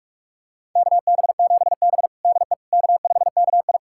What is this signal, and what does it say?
Morse code, heard as a keyed tone, about 46 wpm, 700 Hz. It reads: KB7B DE K5KI
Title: USA Callsign DE Callsign - 4 Characters - 45wpm